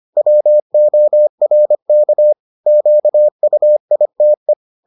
Morse code, heard as a keyed tone, about 25 wpm, 600 Hz. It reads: WORK QUITE